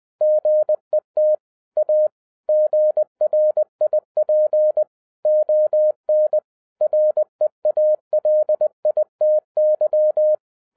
Morse code, heard as a keyed tone, about 20 wpm, 610 Hz. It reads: GET A GRIP ON REALITY